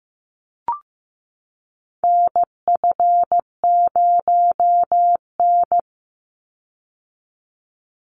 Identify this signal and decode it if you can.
Morse code, heard as a keyed tone, about 15 wpm, 700 Hz. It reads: NF0N
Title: Call Signs Repeated 3x - 15wpm